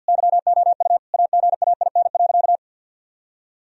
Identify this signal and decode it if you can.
Morse code, heard as a keyed tone, about 47 wpm, 700 Hz. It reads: YOU AGAIN.
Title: Sentences from Top 1000 Words - Review Part 1 - 50wpm